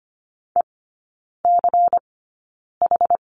Morse code, heard as a keyed tone, about 25 wpm, spaced as for 9 wpm, 700 Hz. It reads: ECH
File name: course / 25:10wpm / 093 Lesson 16e - Letter c - Three Characters 25wpm